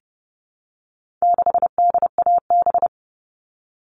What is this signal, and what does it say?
Morse code, heard as a keyed tone, about 30 wpm, 700 Hz. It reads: 6DAB
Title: All Cut Numbers - 4-Digit 30wpm